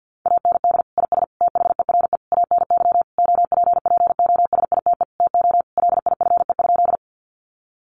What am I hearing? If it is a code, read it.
Morse code, heard as a keyed tone, about 50 wpm, 700 Hz. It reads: AND IS THERE ANY OPPOSITE TO LIFE?
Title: Sentences from Top 1300 Words 50wpm